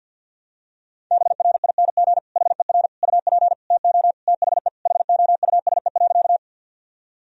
Morse code, heard as a keyed tone, about 53 wpm, 700 Hz. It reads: BRING HER UP TO THE HOUSE.
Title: Sentences from Top 1000 Words - Review Part 3 - 50wpm